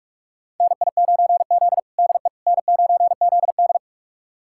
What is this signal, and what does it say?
Morse code, heard as a keyed tone, about 41 wpm, 700 Hz. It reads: NI9Z DE N9ZD
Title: USA Callsign DE Callsign - 4 Characters - 45wpm